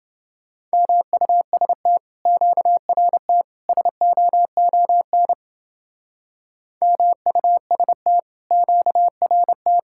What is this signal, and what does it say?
Morse code, heard as a keyed tone, about 29 wpm, 700 Hz. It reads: MUST QRT SOON MUST QRT